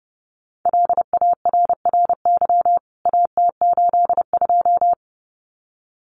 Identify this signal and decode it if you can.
Morse code, heard as a keyed tone, about 30 wpm, 700 Hz. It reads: LARRY AT82
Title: CWT Exchange 30wpm